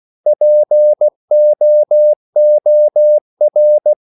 Morse code, heard as a keyed tone, about 16 wpm, 600 Hz. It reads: POOR